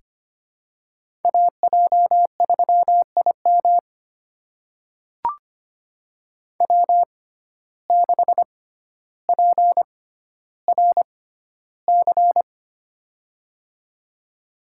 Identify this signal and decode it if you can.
Morse code, heard as a keyed tone, about 25 wpm, spaced as for 13 wpm, 690 Hz. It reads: AJ3IM W6PRC
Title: USA Call Signs - 5 Characters - 6x Character Spacing - 25wpm